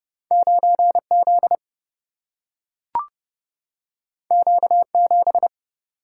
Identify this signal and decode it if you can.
Morse code, heard as a keyed tone, about 30 wpm, 700 Hz. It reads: 9 Z Q 7
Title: Single-Letter Single-Number 30wpm